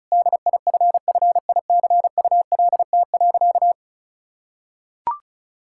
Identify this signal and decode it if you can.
Morse code, heard as a keyed tone, about 36 wpm, 700 Hz. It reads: DIFFICULT.
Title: Sentences from Top 1000 Words - Review Part 5 - 35wpm